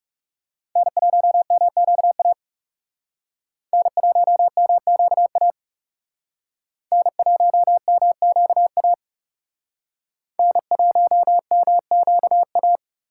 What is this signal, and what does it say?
Morse code, heard as a keyed tone, about 38 wpm, 700 Hz. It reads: N1MQA N1MQA N1MQA N1MQA